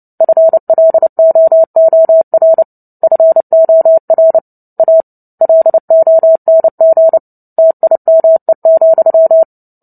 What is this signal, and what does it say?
Morse code, heard as a keyed tone, about 30 wpm, 650 Hz. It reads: FLOOR FOR A LONG TIME,
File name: Book Two / 03-Chapter 3 - 29 WPM